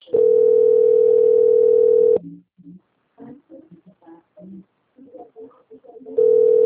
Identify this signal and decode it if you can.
Ringback tone (North America), 440 + 480 Hz.
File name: out- - - - - .25861